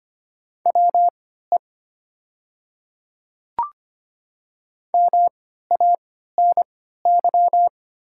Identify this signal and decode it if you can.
Morse code, heard as a keyed tone, about 25 wpm, spaced as for 14 wpm, 700 Hz. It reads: WE MANY
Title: 3x Character Spacing - Top 100 Words - 25wpm